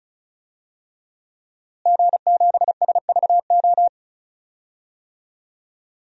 Morse code, heard as a keyed tone, about 36 wpm, 700 Hz. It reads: G7SVO
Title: DXCC-100 Call Signs - 5 Characters - Repeated 3x - 35wpm